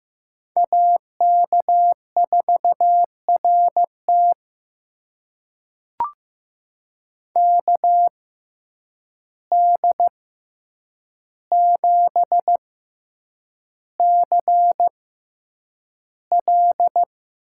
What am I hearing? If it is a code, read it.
Morse code, heard as a keyed tone, about 15 wpm, spaced as for 8 wpm, 700 Hz. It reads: AK4RT KD7CL
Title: USA Call Signs - 5 Characters - 6x Character Spacing - 15wpm